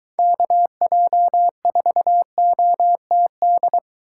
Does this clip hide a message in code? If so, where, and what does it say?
Yes: Morse code from 0.2 to 3.8 s, heard as a keyed tone, about 22 wpm, 700 Hz. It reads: KJ4OTD